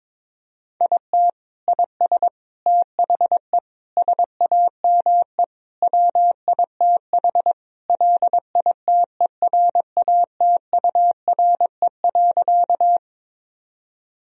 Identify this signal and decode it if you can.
Morse code, heard as a keyed tone, about 21 wpm, 700 Hz. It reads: IT IS THE SAME WITH LITERATURE.
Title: Sentences from Top 1000 Words - Review Part 3 - 22wpm